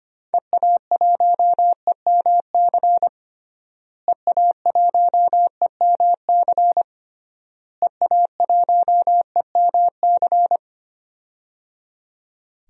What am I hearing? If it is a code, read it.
Morse code, heard as a keyed tone, about 25 wpm, 700 Hz. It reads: EA1EMC EA1EMC EA1EMC
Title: DXCC-100 Call Signs - 6 Characters - Repeated 3x - 25wpm